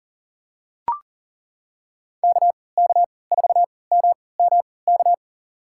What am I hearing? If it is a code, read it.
Morse code, heard as a keyed tone, about 37 wpm, spaced as for 22 wpm, 700 Hz. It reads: KK4MMK
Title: USA Call Signs - 6 Characters - 3x Character Spacing - 40wpm